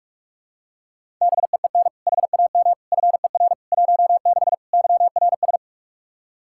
Morse code, heard as a keyed tone, about 46 wpm, 700 Hz. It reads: BEEN HAM FER 16 YRS
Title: Practice QSO - Exchange 3 - More Info - 45wpm